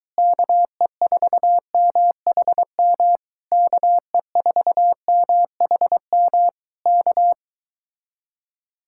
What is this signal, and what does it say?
Morse code, heard as a keyed tone, about 23 wpm, 700 Hz. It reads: KE4MHM KE4MHM K